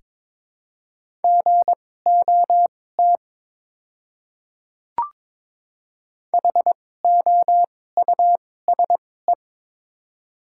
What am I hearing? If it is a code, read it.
Morse code, heard as a keyed tone, about 21 wpm, spaced as for 16 wpm, 700 Hz. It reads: GOT HOUSE
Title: 2x Character Spacing - Top 300 Words - 22wpm